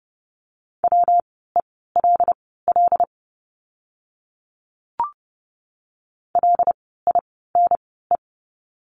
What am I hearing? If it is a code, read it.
Morse code, heard as a keyed tone, about 30 wpm, spaced as for 17 wpm, 700 Hz. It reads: WELL LINE
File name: course / 30:10wpm / 115 Lesson 19f - Letter w - Words 30wpm